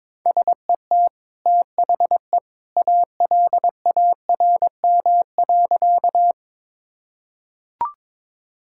Morse code, heard as a keyed tone, about 22 wpm, 700 Hz. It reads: SET THE ALARM.
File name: practice / 22wpm / Easy Sentences with Hints - 22wpm